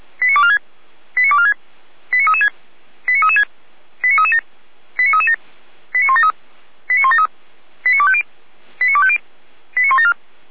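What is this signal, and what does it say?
Five-tone selective call (ZVEI), heard as quick runs of five tones, tones of about 70 ms. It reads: 89246 89246 89336 89336 89337 89337 89173 89173 89250 89250 89174
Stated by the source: ZVEI five-tone call received by SDR, from the feuerwehr-orlamuende.org fire brigade audio archive.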